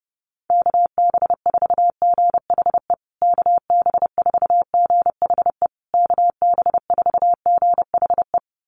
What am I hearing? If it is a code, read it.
Morse code, heard as a keyed tone, about 30 wpm, 700 Hz. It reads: KB4GHE KB4GHE KB4GHE
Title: USA Call Signs Repeated 3x 30wpm